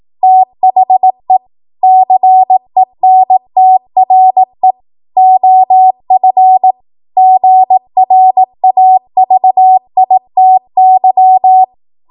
Morse code, heard as a keyed tone, about 18 wpm, 750 Hz. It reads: THE CENTRE OF GRAVITY
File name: Morse / 18WPM